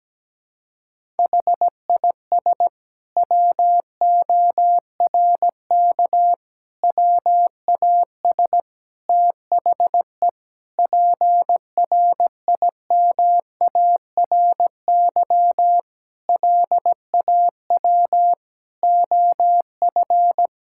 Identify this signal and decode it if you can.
Morse code, heard as a keyed tone, about 17 wpm, 700 Hz. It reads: HIS WORK WAS THE PRIMARY LAW OF
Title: Sentences from Top 1000 Words - Review Part 5 - 17wpm